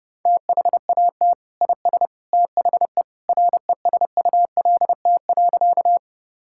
Morse code, heard as a keyed tone, about 32 wpm, 700 Hz. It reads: THAT IS THE RESULT.
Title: Sentences from Top 1000 Words - Review Part 5 - 30wpm